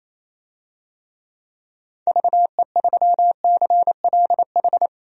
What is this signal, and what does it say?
Morse code, heard as a keyed tone, about 28 wpm, 700 Hz. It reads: VE3CLH
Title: DXCC-100 Call Signs - 6 Characters - 28wpm